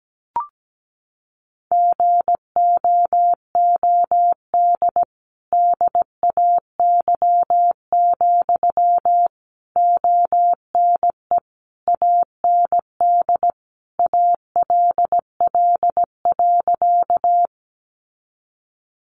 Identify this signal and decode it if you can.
Morse code, heard as a keyed tone, about 17 wpm, 700 Hz. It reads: GOOD DAY, ONE AND ALL.
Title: Sentences from Top 200 Words 17wpm